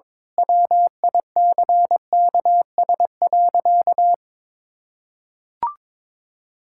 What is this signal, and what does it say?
Morse code, heard as a keyed tone, about 21 wpm, 700 Hz. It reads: WICKS.
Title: Pangram Challenge v2 - 22wpm